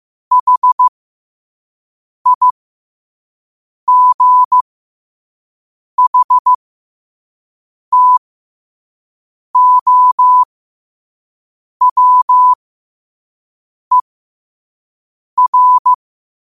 Morse code, heard as a keyed tone, about 15 wpm, spaced as for 5 wpm, 1000 Hz. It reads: HIGHTOWER
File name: qotdfiles / 5wpm